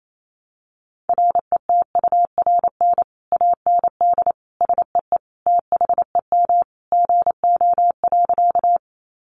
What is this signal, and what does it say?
Morse code, heard as a keyed tone, about 28 wpm, 700 Hz. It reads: RETURN AND SEE THEM GO.
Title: Sentences from Top 800 Words 28wpm